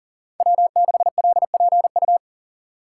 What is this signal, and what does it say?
Morse code, heard as a keyed tone, about 39 wpm, 700 Hz. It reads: W6LPU
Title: USA Call Signs - 5 Characters - 2x Character Spacing - 40wpm